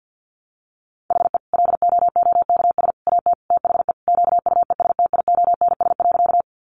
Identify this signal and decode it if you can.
Morse code, heard as a keyed tone, about 50 wpm, 700 Hz. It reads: HE LOOKS AT THE QUESTIONS.
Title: Additional Easy Phrases 50wpm